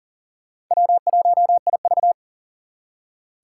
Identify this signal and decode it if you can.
Morse code, heard as a keyed tone, about 40 wpm, 700 Hz. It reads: W1IV
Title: Four Challenging Letter-Number Combinations 40wpm